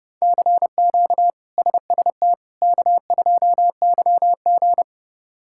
Morse code, heard as a keyed tone, about 30 wpm, 700 Hz. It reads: CQ SST K2YG
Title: K1USN SST QSO Practice - 30wpm